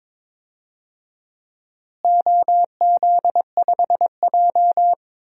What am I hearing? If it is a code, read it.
Morse code, heard as a keyed tone, about 22 wpm, 700 Hz. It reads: OZ5J
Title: DXCC-100 Call Signs - 4 Characters - Repeated 2x - 22wpm